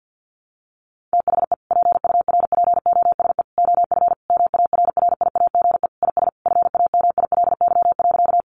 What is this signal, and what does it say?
Morse code, heard as a keyed tone, about 50 wpm, 700 Hz. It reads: THE PURPOSE OF MARRIAGE IS FAMILY.
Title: Sentences from Top 800 Words 50wpm